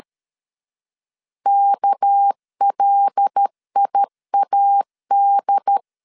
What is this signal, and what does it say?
Morse code, heard as a keyed tone, about 13 wpm, 800 Hz. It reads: KLIAD
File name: LEZIONI / 6 PKX0 / 6 PKX0 12wpm 2231